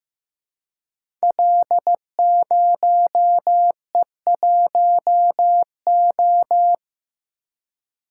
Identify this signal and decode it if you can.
Morse code, heard as a keyed tone, about 15 wpm, 700 Hz. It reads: L0E1O
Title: Code Groups - 5 Letter:Numbers - 15wpm